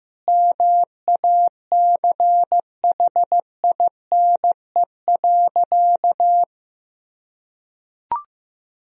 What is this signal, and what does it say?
Morse code, heard as a keyed tone, about 15 wpm, 700 Hz. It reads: MACHINE.
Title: Sentences from Top 1000 Words - Review Part 3 - 15wpm